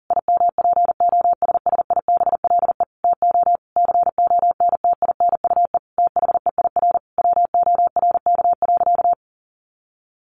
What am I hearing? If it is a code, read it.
Morse code, heard as a keyed tone, about 40 wpm, 700 Hz. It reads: IMPOSSIBLE TO CONTINUE THEIR WORK.
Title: Sentences from Top 1100 Words 40wpm